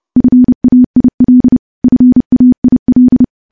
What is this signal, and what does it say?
Morse code, heard as a keyed tone, about 30 wpm, 250 Hz. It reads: FAIL FAIL
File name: morse / morse-training-30WPM